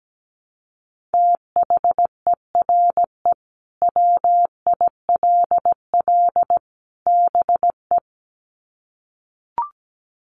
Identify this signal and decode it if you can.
Morse code, heard as a keyed tone, about 17 wpm, 700 Hz. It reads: THERE WILL BE
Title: Sets of 3 Words 17wpm